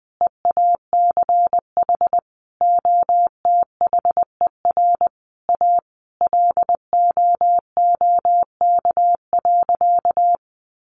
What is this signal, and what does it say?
Morse code, heard as a keyed tone, about 20 wpm, 700 Hz. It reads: EACH OTHER A LOOK.
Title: A Sound Of Thunder 20wpm